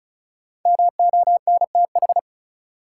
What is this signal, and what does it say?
Morse code, heard as a keyed tone, about 34 wpm, 700 Hz. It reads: MONTH
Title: Sets of 5 Words Encore - New Word Drill - 35wpm